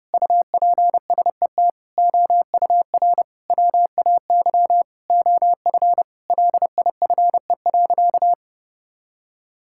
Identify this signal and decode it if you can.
Morse code, heard as a keyed tone, about 30 wpm, 700 Hz. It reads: UPSET OUR WAY OF LIFE.
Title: Sentences from Top 1000 Words - Review Part 6 - 30wpm